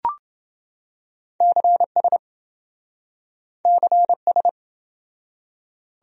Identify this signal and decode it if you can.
Morse code, heard as a keyed tone, about 27 wpm, 700 Hz. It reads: CS CS